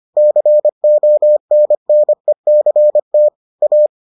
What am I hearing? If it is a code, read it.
Morse code, heard as a keyed tone, about 25 wpm, 600 Hz. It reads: CONNECT A